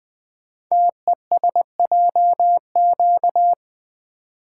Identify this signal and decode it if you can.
Morse code, heard as a keyed tone, about 20 wpm, 700 Hz. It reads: TESJQ
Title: Code Groups - 5 Letters - 20wpm